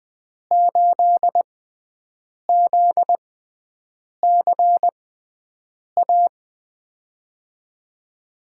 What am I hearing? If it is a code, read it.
Morse code, heard as a keyed tone, about 20 wpm, spaced as for 7 wpm, 700 Hz. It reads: 8ZCA
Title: USA Call Signs - 5 Characters - 6x Character Spacing - 20wpm